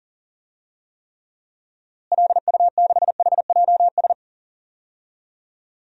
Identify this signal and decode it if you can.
Morse code, heard as a keyed tone, about 37 wpm, 700 Hz. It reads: LU6HJS
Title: DXCC-100 Call Signs - 6 Characters - Repeated 3x - 40wpm